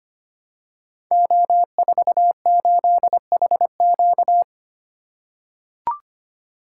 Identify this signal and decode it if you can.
Morse code, heard as a keyed tone, about 25 wpm, 700 Hz. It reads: O48HQ
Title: Code Groups - 5 Letter:Numbers - 25wpm